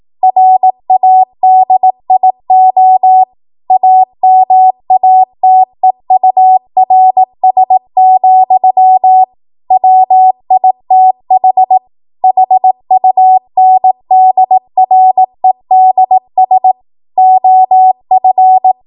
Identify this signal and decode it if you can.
Morse code, heard as a keyed tone, about 18 wpm, 750 Hz. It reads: RADIO AMATEURS, WITH HUNDREDS OF
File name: Morse / 18WPM